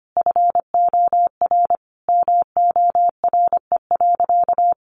Morse code, heard as a keyed tone, about 25 wpm, 700 Hz. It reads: FOR MORE.
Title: Sentences from Top 1100 Words 25wpm